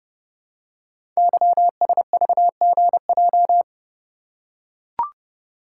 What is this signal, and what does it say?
Morse code, heard as a keyed tone, about 30 wpm, 700 Hz. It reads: YSVGJ
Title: Code Groups - 5 Letters - 30wpm